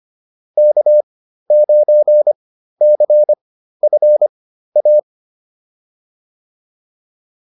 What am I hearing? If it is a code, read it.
Morse code, heard as a keyed tone, about 25 wpm, spaced as for 13 wpm, 600 Hz. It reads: K9CFA
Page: intermediate-practice-files